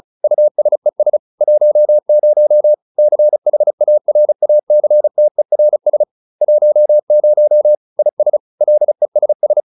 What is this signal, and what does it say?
Morse code, heard as a keyed tone, about 34 wpm, 600 Hz. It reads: USES 10 CHARACTERS 10 IS LESS